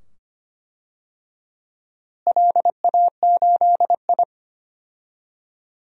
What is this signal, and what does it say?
Morse code, heard as a keyed tone, about 25 wpm, 700 Hz. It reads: LA8I
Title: DXCC-100 Call Signs - 4 Characters - Repeated 2x - 25wpm